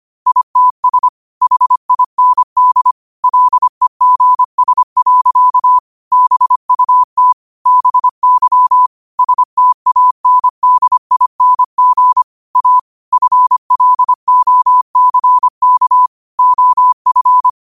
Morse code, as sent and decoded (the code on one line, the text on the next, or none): .. - ... / .... .. -. -.. / .-.. --. ... .-.-.- / -... ..- - / -... -.-- / ... - .- -. -.. .. -. --. / .- / ..-. .-.. --- -.-. -.- / --- ..-.
ITS HIND LEGS. BUT BY STANDING A FLOCK OF